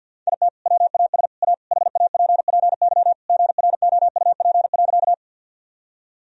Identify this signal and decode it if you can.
Morse code, heard as a keyed tone, about 48 wpm, 700 Hz. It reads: IT WAS A HAPPY GROUP.
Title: Sentences from Top 1000 Words - Review Part 1 - 50wpm